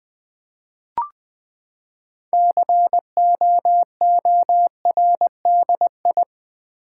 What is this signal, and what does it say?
Morse code, heard as a keyed tone, about 20 wpm, 700 Hz. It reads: COORDI
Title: Build A Word - 6 Letters At A Time - 12-Letter Words - 20wpm